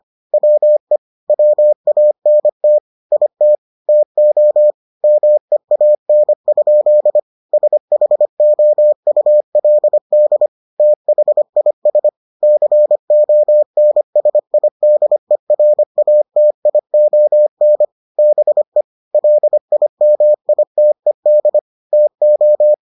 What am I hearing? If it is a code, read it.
Morse code, heard as a keyed tone, about 25 wpm, 600 Hz. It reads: WE WANT IT TO MEAN? SHOULD THIS CONSIDERATION BE LIMITED TO